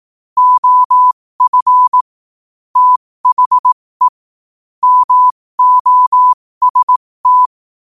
Morse code, heard as a keyed tone, about 18 wpm, spaced as for 15 wpm, 1000 Hz. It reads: OF THE MOST
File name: qotdfiles / 15wpm